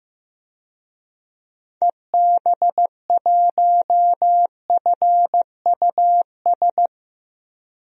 Morse code, heard as a keyed tone, about 15 wpm, 700 Hz. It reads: EB1FUS
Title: DXCC-100 Call Signs - 6 Characters - 15wpm